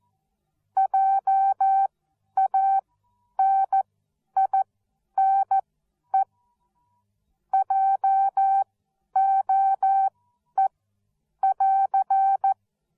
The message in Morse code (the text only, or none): JANINE JOE<AR>